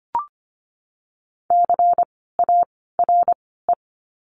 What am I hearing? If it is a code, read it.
Morse code, heard as a keyed tone, about 24 wpm, 700 Hz. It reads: C A R E
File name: course / 25:10wpm / 094 Lesson 16f - Letter c - Words 25wpm